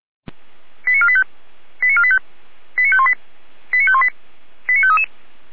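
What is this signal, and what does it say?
Five-tone selective call (ZVEI), heard as quick runs of five tones, tones of about 70 ms. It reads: 89485 89485 89518 89518 89533
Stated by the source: ZVEI five-tone call 89456 received by SDR.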